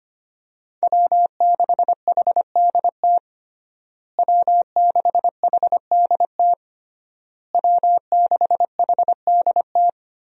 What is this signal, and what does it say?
Morse code, heard as a keyed tone, about 25 wpm, 700 Hz. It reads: W6HDT W6HDT W6HDT
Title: USA Call Signs - 5 Characters - Repeated 3x - 25wpm